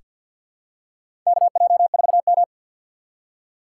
Morse code, heard as a keyed tone, about 47 wpm, 700 Hz. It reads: KJ4M
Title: USA Call Signs - 2x Character Spacing - 50wpm